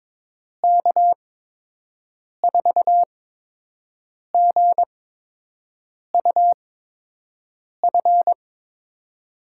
Morse code, heard as a keyed tone, about 22 wpm, spaced as for 6 wpm, 700 Hz. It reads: K4GUF
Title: USA Call Signs - 6 Characters - 8x Character Spacing - 22wpm